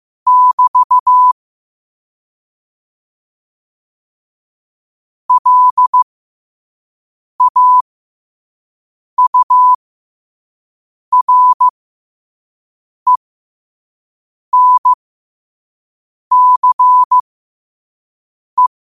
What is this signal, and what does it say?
Morse code, heard as a keyed tone, about 15 wpm, spaced as for 5 wpm, 1000 Hz. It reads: <BT> LAURENCE
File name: 5wpm